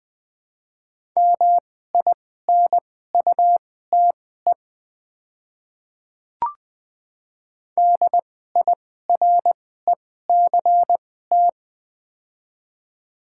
Morse code, heard as a keyed tone, about 20 wpm, spaced as for 14 wpm, 690 Hz. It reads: MINUTE DIRECT
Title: 2x Character Spacing - Top 500 Words - 20wpm